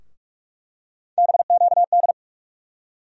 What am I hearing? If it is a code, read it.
Morse code, heard as a keyed tone, about 45 wpm, 700 Hz. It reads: BQD
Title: USA Call Sign Postfixes - 3 Letters 45wpm